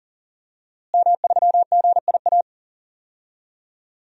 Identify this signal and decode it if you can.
Morse code, heard as a keyed tone, about 40 wpm, 700 Hz. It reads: M3GIA
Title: DXCC-100 Call Signs - 5 Characters - Repeated 2x - 40wpm